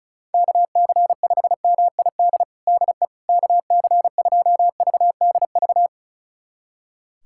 Morse code, heard as a keyed tone, about 34 wpm, 700 Hz. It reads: KC5MID DE KC2VDV
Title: USA Callsign DE Callsign - 6 Characters - 35wpm